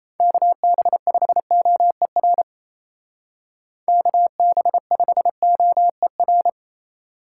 Morse code, heard as a keyed tone, about 33 wpm, 700 Hz. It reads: KB5OER KB5OER
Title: Speed Racing - USA 2x3 Call Signs 28wpm